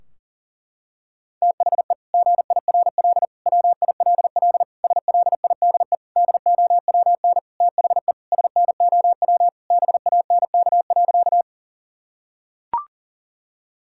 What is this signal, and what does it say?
Morse code, heard as a keyed tone, about 40 wpm, 700 Hz. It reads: THE GIRL WILL SLIDE DOWN THE SNOW BANK.
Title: Sentences from Top 1600 Words 40wpm